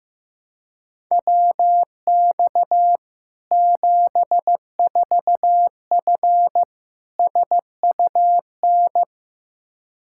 Morse code, heard as a keyed tone, about 16 wpm, 700 Hz. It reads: WX 74F SUN
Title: Practice QSO - Exchange 2 - Info - 15wpm